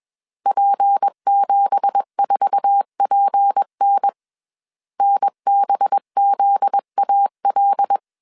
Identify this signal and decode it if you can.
Morse code, heard as a keyed tone, about 21 wpm, 800 Hz. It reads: P74PN NBZAL
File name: CW / DET / 1 9 1500C S C 20wpm 2310